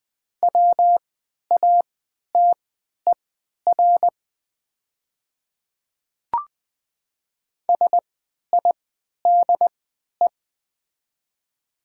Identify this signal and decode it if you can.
Morse code, heard as a keyed tone, about 20 wpm, spaced as for 11 wpm, 700 Hz. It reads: WATER SIDE
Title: 3x Character Spacing - Top 100 Words - 20wpm